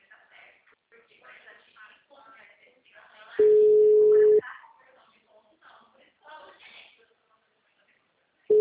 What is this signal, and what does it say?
Ringback tone (Europe, 425 Hz), with speech.